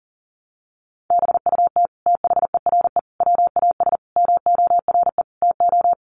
Morse code, heard as a keyed tone, about 40 wpm, 700 Hz. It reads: BUT THERE WAS MORE TO
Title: Sentences from Top 100 Words 40wpm